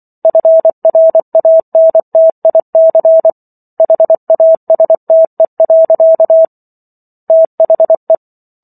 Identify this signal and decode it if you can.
Morse code, heard as a keyed tone, about 24 wpm, 650 Hz. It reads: FRANTIC HASTE. THE